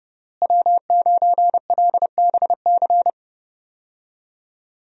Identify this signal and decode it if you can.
Morse code, heard as a keyed tone, about 30 wpm, 700 Hz. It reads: W9LBC
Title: USA Call Signs - 5 Characters - Repeated 3x - 30wpm